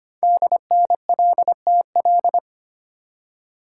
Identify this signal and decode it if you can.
Morse code, heard as a keyed tone, about 25 wpm, 700 Hz. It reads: DNLTL
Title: Code Groups - 5 Letters - 25wpm